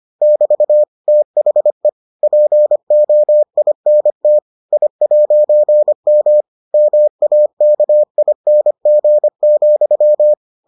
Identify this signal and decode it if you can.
Morse code, heard as a keyed tone, about 25 wpm, 600 Hz. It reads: <BT> THE POINT I'M MAKING,